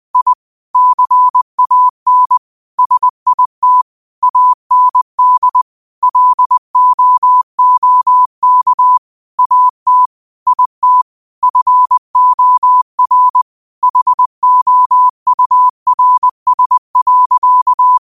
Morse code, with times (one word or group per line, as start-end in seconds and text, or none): 0.1-0.3 I
0.7-2.4 CAN
2.8-3.8 SIT
4.2-5.6 AND
6.0-9.0 LOOK
9.4-10.1 AT
10.5-11.0 IT
11.4-13.4 FOR
13.8-18.0 HOURS.